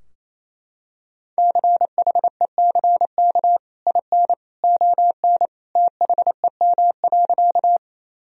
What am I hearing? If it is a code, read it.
Morse code, heard as a keyed tone, about 28 wpm, 700 Hz. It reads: CHECK IN ON THEM.
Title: Sentences from Top 1000 Words - Review Part 3 - 28wpm